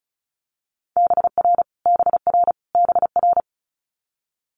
Morse code, heard as a keyed tone, about 36 wpm, 700 Hz. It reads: BR BR BR
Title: Multi-letter Phonemes Repeated 3x 35wpm